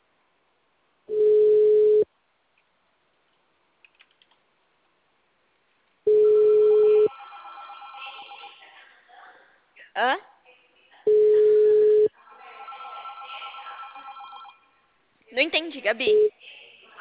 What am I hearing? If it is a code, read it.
Ringback tone (Europe, 425 Hz), with speech.